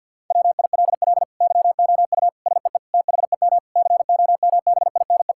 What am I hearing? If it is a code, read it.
Morse code, heard as a keyed tone, about 50 wpm, 700 Hz. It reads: WILL YOU SEE THEM COMBINE
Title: Sentences from Top 1600 Words 50wpm